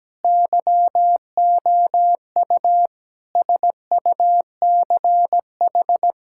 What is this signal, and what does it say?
Morse code, heard as a keyed tone, about 17 wpm, 700 Hz. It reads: YOU SUCH
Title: Sentences from Top 1600 Words 17wpm